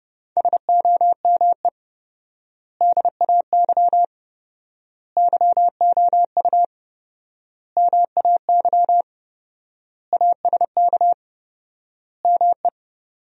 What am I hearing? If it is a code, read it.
Morse code, heard as a keyed tone, about 30 wpm, 700 Hz. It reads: SOME DAY YOU MAY ASK ME
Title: 4x Word-Spacing - Sentences from Top 200 Words - 30wpm